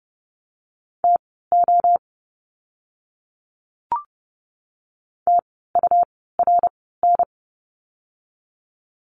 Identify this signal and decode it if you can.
Morse code, heard as a keyed tone, about 30 wpm, spaced as for 17 wpm, 700 Hz. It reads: TO TURN
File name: Lesson 17f - Letter u - Words 30wpm